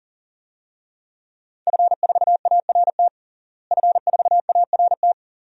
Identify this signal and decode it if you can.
Morse code, heard as a keyed tone, about 40 wpm, 700 Hz. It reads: F4ART F4ART
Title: DXCC-100 Call Signs - 5 Characters - Repeated 2x - 40wpm